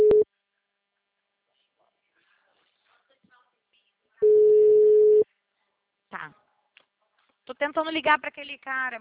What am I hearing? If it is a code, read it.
Ringback tone (Europe, 425 Hz), with speech.